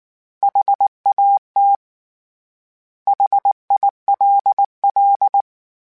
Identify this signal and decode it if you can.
Morse code, heard as a keyed tone, about 19 wpm, 800 Hz. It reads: HAT HILL
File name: H Prefixes / H mp3s